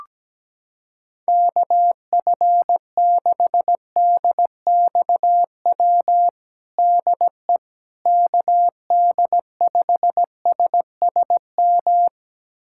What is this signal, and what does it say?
Morse code, heard as a keyed tone, about 17 wpm, 700 Hz. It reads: KF6DXW DE KD5SSM
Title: USA Callsign DE Callsign - 6 Characters - 17wpm